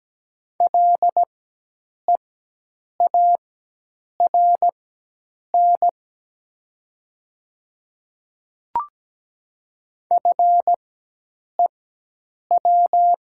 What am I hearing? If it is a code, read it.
Morse code, heard as a keyed tone, about 17 wpm, spaced as for 8 wpm, 700 Hz. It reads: LEARN FEW
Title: 4x Character Spacing - Top 300 Words - 17wpm